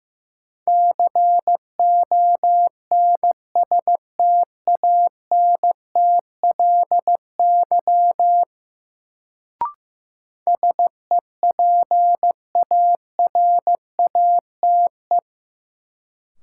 Morse code, heard as a keyed tone, about 15 wpm, 700 Hz. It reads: CONSTANTLY SEPARATE
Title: Sentences from Top 1000 Words - Review Part 7 - New Word Drill - 15wpm